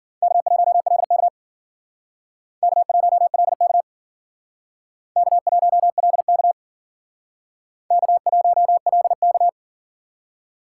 Morse code, heard as a keyed tone, about 52 wpm, 700 Hz. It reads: K1LK K1LK K1LK K1LK